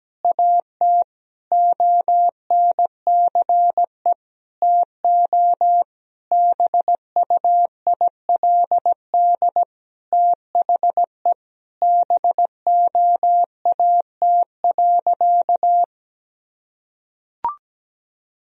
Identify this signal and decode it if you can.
Morse code, heard as a keyed tone, about 17 wpm, 700 Hz. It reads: AT ONCE TO BUILD THE BOAT.
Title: Sentences from Top 1000 Words - Review Part 1 - 17wpm